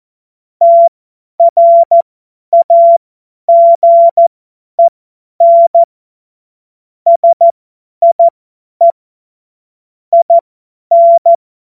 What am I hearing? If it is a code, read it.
Morse code, heard as a keyed tone, about 14 wpm, spaced as for 10 wpm, 680 Hz. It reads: TRAGEN SIE IN
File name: Morsekurs / Witze / Witz 02. Februar 2021, 50 F70